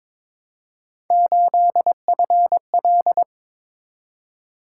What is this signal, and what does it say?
Morse code, heard as a keyed tone, about 21 wpm, 700 Hz. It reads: 8FL
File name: practice / 22wpm / Three Letter-Number Combinations - 22wpm